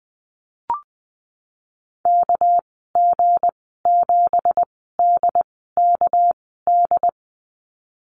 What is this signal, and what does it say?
Morse code, heard as a keyed tone, about 20 wpm, spaced as for 14 wpm, 700 Hz. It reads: KG7DKD
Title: 245 Lesson 39g - Letter z - Call Signs 20wpm